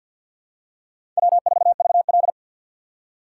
Morse code, heard as a keyed tone, about 51 wpm, 700 Hz. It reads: W4VL
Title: USA Call Signs - 4 Characters - 3x Character Spacing - 50wpm